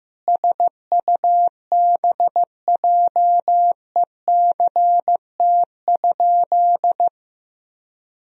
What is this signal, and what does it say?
Morse code, heard as a keyed tone, about 15 wpm, 700 Hz. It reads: SUBJECT?
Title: Sentences from Top 1000 Words - Review Part 3 - 15wpm